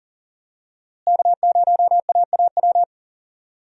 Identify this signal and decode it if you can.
Morse code, heard as a keyed tone, about 40 wpm, 700 Hz. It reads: K0AAW
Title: USA Call Signs - 5 Characters - Repeated 2x - 40wpm